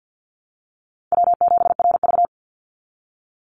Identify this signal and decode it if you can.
Morse code, heard as a keyed tone, about 50 wpm, 700 Hz. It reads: W7RV
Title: NA SOTA Chaser Callsigns 50wpm